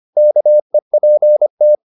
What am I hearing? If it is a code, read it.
Morse code, heard as a keyed tone, about 25 wpm, 600 Hz. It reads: KEPT
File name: the old man and the sea part 1 25wpm